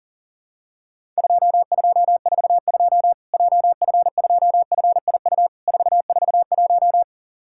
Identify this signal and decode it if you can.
Morse code, heard as a keyed tone, about 40 wpm, 700 Hz. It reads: 2242 JF2FIU 441
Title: Worked All Europe DX Contest - QTC Exchange - 40wpm